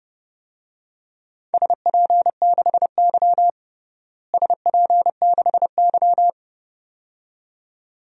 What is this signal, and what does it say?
Morse code, heard as a keyed tone, about 30 wpm, 700 Hz. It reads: SP6Y SP6Y
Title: DXCC-100 Call Signs - 4 Characters - Repeated 2x - 30wpm